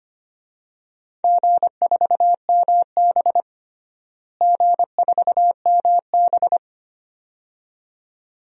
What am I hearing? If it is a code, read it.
Morse code, heard as a keyed tone, about 25 wpm, 700 Hz. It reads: G4MB G4MB
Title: DXCC-100 Call Signs - 4 Characters - Repeated 2x - 25wpm